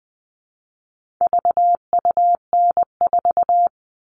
Morse code, heard as a keyed tone, about 20 wpm, 700 Hz. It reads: VUN4
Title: All Cut Numbers - 4-Digit 20wpm